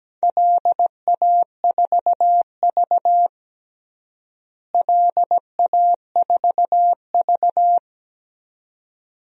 Morse code, heard as a keyed tone, about 17 wpm, 700 Hz. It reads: LA4V LA4V
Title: DXCC-100 Call Signs - 4 Characters - Repeated 2x - 17wpm